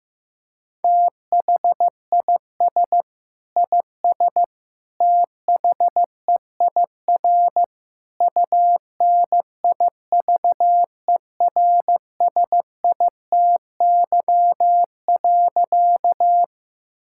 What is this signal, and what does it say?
Morse code, heard as a keyed tone, about 15 wpm, 700 Hz. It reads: THIS IS THEIR UNIVERSITY.
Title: Sentences from Top 1000 Words - Review Part 5 - 15wpm